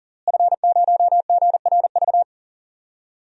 Morse code, heard as a keyed tone, about 40 wpm, 700 Hz. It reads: F0GRV
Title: DXCC-100 Call Signs - 5 Characters - Repeated 2x - 40wpm